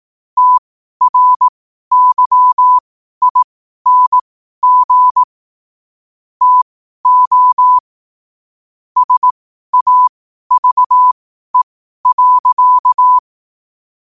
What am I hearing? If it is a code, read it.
Morse code, heard as a keyed tone, about 18 wpm, spaced as for 12 wpm, 1000 Hz. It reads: TRYING TO SAVE.
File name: qotdfiles / 12wpm